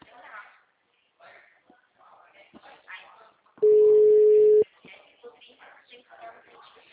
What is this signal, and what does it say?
Ringback tone (Europe, 425 Hz), with speech.
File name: out- -2012- - - .66360